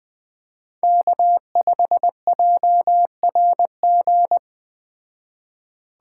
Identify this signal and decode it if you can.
Morse code, heard as a keyed tone, about 20 wpm, 700 Hz. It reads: K5JRG
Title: Speed Racing - USA 1x3 Call Signs 17wpm